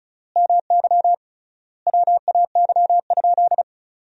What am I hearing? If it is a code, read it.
Morse code, heard as a keyed tone, about 34 wpm, 700 Hz. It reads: MY WAY?
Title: 3x Word-Spacing - Sentences from Top 200 Words - 35wpm